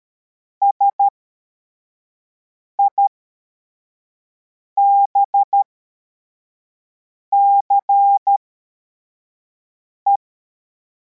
Morse code, heard as a keyed tone, about 13 wpm, spaced as for 4 wpm, 800 Hz. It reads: SIBCE
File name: formation / cw / Text52